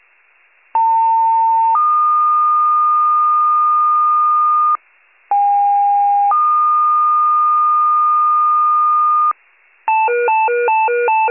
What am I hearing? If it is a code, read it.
Two-tone page, heard as a one-second tone, then a three-second tone, about 900 then 1220 Hz, about 810 then 1220 Hz. It is heard from 0.7 to 9.3 s.
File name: wcalert